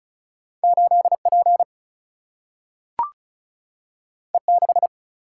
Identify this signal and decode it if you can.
Morse code, heard as a keyed tone, about 36 wpm, 700 Hz. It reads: 8 P E 6
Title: Single-Letter Single-Number 35wpm